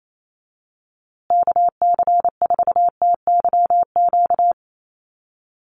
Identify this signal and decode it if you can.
Morse code, heard as a keyed tone, about 28 wpm, 700 Hz. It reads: KC4TYQ
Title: USA Call Signs Repeated 3x 28wpm